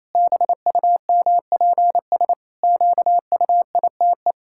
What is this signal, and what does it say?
Morse code, heard as a keyed tone, about 28 wpm, 700 Hz. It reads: BUMPS QUITE
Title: Pangram Challenge v2 - 28wpm